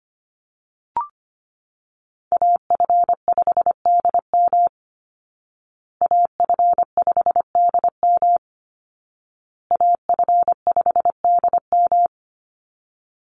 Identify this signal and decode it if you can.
Morse code, heard as a keyed tone, about 25 wpm, 700 Hz. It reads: AF5DM AF5DM AF5DM
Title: Call Signs Repeated 2x - 25wpm